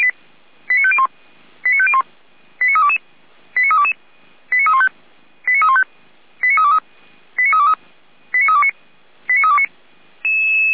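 Five-tone selective call (ZVEI), heard as quick runs of five tones, tones of about 70 ms. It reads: 89501 89501 89322 89322 89315 89315 89323 89323 89329 89329
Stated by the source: Five-tone ZVEI selective call, received with an SDR.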